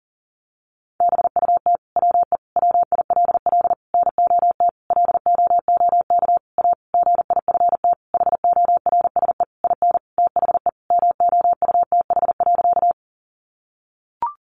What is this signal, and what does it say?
Morse code, heard as a keyed tone, about 40 wpm, 700 Hz. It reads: BUT WE WILL NOT LOOK A GIFT HORSE IN THE MOUTH.
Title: Sentences from Top 1500 Words 40wpm